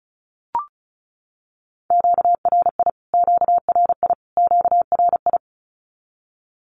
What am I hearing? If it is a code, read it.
Morse code, heard as a keyed tone, about 34 wpm, 700 Hz. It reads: QRI QRI QRI
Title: Q-Codes Repeated 3x 35wpm